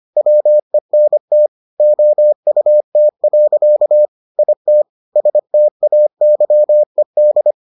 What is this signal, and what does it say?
Morse code, heard as a keyed tone, about 25 wpm, 600 Hz. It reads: WENT OUT. IT STAYED